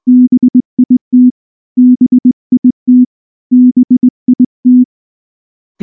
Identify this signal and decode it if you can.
Morse code, heard as a keyed tone, about 20 wpm, 250 Hz. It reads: BIT BIT BIT
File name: morse-training-20WPM